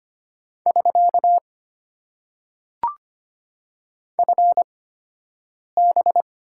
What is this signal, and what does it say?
Morse code, heard as a keyed tone, about 25 wpm, spaced as for 7 wpm, 700 Hz. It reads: <SK> FB
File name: Top 90 QSO Elements - 8x Character Spacing - 25wpm